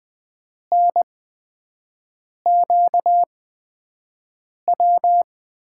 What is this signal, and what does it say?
Morse code, heard as a keyed tone, about 20 wpm, spaced as for 5 wpm, 700 Hz. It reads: NQW
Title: USA Call Signs - 6 Characters - 8x Character Spacing - 20wpm